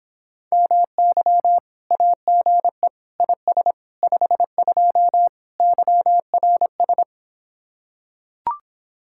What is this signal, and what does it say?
Morse code, heard as a keyed tone, about 26 wpm, 700 Hz. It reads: MY AGE IS 52 YRS